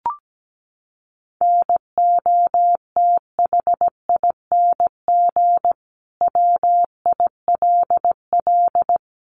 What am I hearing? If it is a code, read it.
Morse code, heard as a keyed tone, about 17 wpm, 700 Hz. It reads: NOTHING WILL
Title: Sentences from Top 1400 Words 17wpm